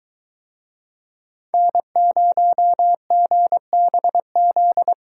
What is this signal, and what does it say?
Morse code, heard as a keyed tone, about 23 wpm, 700 Hz. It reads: N0GBZ